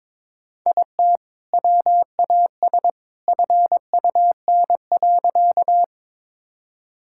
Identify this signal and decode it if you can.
Morse code, heard as a keyed tone, about 22 wpm, 700 Hz. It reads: IT WAS FUN.
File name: practice / 22wpm / Sentences with 2-Word Phrases - 22wpm